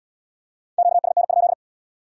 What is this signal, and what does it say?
Morse code, heard as a keyed tone, about 75 wpm, 700 Hz. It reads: QSN?